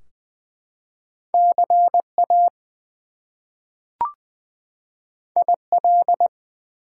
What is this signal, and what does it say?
Morse code, heard as a keyed tone, about 20 wpm, 700 Hz. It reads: CA IL
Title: US States and Canadian Provinces 20wpm